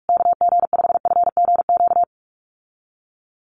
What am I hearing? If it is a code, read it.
Morse code, heard as a keyed tone, about 44 wpm, 690 Hz. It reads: KG5FGQ
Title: USA Call Signs Repeated 3x 45wpm